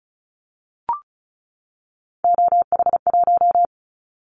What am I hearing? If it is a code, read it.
Morse code, heard as a keyed tone, about 36 wpm, 700 Hz. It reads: OH1
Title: Callsign Prefixes for Top 25 DXCC - 35wpm